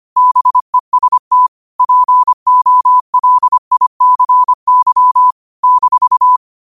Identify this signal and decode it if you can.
Morse code, heard as a keyed tone, about 25 wpm, 1000 Hz. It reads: BEST POLICY -